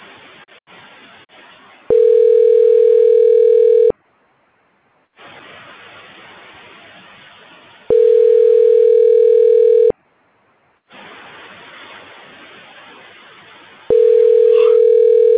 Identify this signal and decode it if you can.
Ringback tone (North America), 440 + 480 Hz.